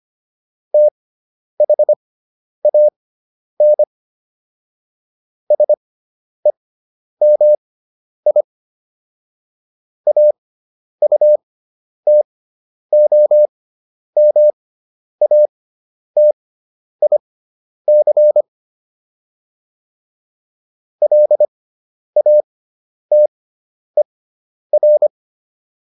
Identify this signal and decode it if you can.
Morse code, heard as a keyed tone, about 25 wpm, spaced as for 10 wpm, 600 Hz. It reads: THAN SEMI AUTOMATIC LATER